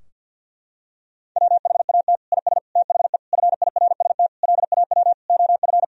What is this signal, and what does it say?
Morse code, heard as a keyed tone, about 52 wpm, 700 Hz. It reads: WHAT IS THE FIRST LAW OF
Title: Sentences from Top 1000 Words - Review Part 3 - 50wpm